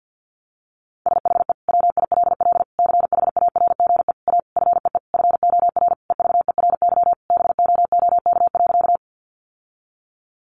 Morse code, heard as a keyed tone, about 50 wpm, 700 Hz. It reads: SHE WILL CHARGE A FEE FOR EVERY BOOK.
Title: Sentences from Top 1500 Words 50wpm